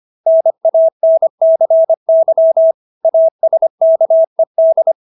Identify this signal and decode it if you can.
Morse code, heard as a keyed tone, about 25 wpm, 650 Hz. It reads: NANCY ASKED